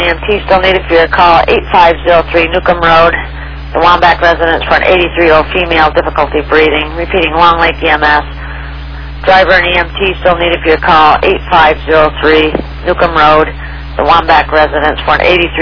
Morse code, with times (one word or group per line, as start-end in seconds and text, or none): none